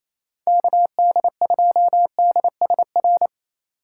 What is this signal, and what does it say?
Morse code, heard as a keyed tone, about 27 wpm, 700 Hz. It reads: KD2DSR
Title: USA Call Signs - 6 Characters - 4x Character Spacing - 28wpm